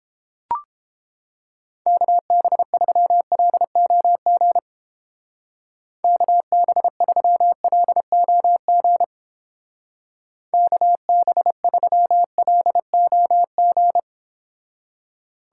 Morse code, heard as a keyed tone, about 31 wpm, 700 Hz. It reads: KB3LOG KB3LOG KB3LOG